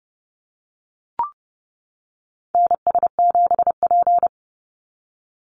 Morse code, heard as a keyed tone, about 30 wpm, 700 Hz. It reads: NS7P
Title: NA SOTA Activator Callsigns 30wpm